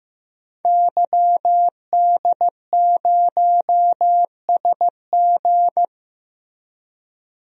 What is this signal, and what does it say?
Morse code, heard as a keyed tone, about 15 wpm, 700 Hz. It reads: YD0SG
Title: DXCC-100 Call Signs - 5 Characters - 15wpm